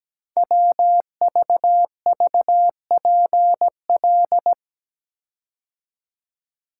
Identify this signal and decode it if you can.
Morse code, heard as a keyed tone, about 17 wpm, 700 Hz. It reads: WVVPL
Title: Code Groups - 5 Letter:Numbers - 17wpm